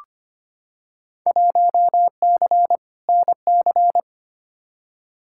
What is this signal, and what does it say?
Morse code, heard as a keyed tone, about 25 wpm, 700 Hz. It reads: 1C NC
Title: Field Day Exchanges - 25wpm